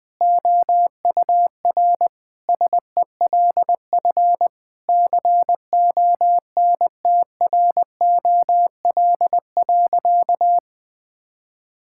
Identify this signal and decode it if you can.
Morse code, heard as a keyed tone, about 20 wpm, 700 Hz. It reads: OUR SELF CONTROL.
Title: Sentences from Top 1000 Words - Review Part 5 - 20wpm